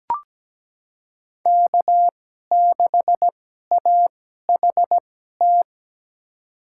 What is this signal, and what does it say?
Morse code, heard as a keyed tone, about 17 wpm, spaced as for 12 wpm, 700 Hz. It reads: K6AHT
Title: USA Call Signs - 5 Characters - 2x Character Spacing - 17wpm